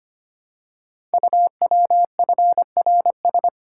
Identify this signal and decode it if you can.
Morse code, heard as a keyed tone, about 25 wpm, 700 Hz. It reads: UWFRS
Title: Code Groups - 5 Letters - 25wpm